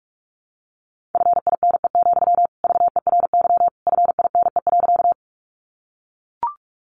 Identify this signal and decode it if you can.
Morse code, heard as a keyed tone, about 44 wpm, 700 Hz. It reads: FINE, VERY FINE.
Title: Sentences from Top 500 Words 45wpm